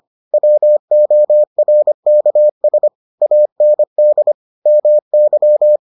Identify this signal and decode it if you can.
Morse code, heard as a keyed tone, about 25 wpm, 600 Hz. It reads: WORKS AND MY